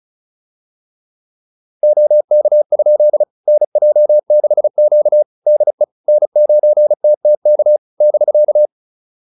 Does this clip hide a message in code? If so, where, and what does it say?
Yes: Morse code from 1.8 to 8.7 s, heard as a keyed tone, about 34 wpm, 600 Hz. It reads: OK? NJ6Q DE N9TTK <BK>